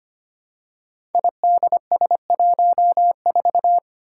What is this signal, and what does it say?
Morse code, heard as a keyed tone, about 25 wpm, 700 Hz. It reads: IDS14
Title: Five Challenging Letter-Number Combinations 25wpm